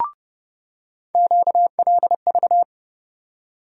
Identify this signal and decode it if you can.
Morse code, heard as a keyed tone, about 30 wpm, 700 Hz. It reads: QLV
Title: USA Call Sign Postfixes - 3 Letters 30wpm